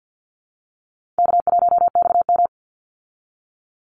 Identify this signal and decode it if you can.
Morse code, heard as a keyed tone, about 49 wpm, 700 Hz. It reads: K1XM
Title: Call Signs Repeated 3x - 50wpm